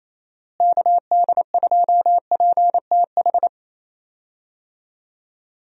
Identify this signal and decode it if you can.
Morse code, heard as a keyed tone, about 27 wpm, 700 Hz. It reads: KD2PTH
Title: USA Call Signs - 6 Characters - Repeated 3x - 28wpm